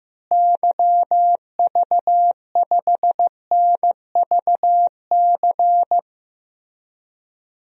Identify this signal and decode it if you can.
Morse code, heard as a keyed tone, about 15 wpm, 700 Hz. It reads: YV5NVC
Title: DXCC-100 Call Signs - 6 Characters - Repeated 3x - 15wpm